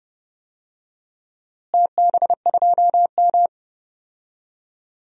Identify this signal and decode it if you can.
Morse code, heard as a keyed tone, about 32 wpm, 700 Hz. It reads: TB2M
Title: DXCC-100 Call Signs - 4 Characters - Repeated 2x - 30wpm